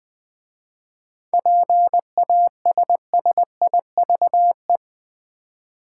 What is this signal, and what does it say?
Morse code, heard as a keyed tone, about 20 wpm, 700 Hz. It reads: PASSIVE
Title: 7 Letter Words - First 3 Letters Given - 20wpm